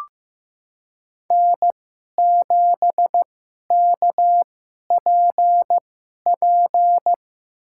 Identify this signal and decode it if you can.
Morse code, heard as a keyed tone, about 15 wpm, spaced as for 11 wpm, 700 Hz. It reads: N7KPP
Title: USA Call Signs - 5 Characters - 2x Character Spacing - 15wpm